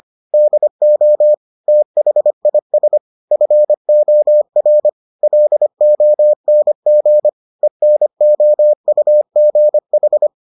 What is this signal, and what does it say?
Morse code, heard as a keyed tone, about 25 wpm, 600 Hz. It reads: DO THIS FOR LONG ENOUGH